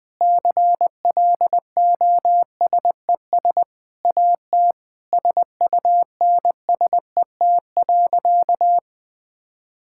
Morse code, heard as a keyed tone, about 20 wpm, 700 Hz. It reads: CLOSES AT SUNSET.
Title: Sentences with 6-Word Phrases - 20wpm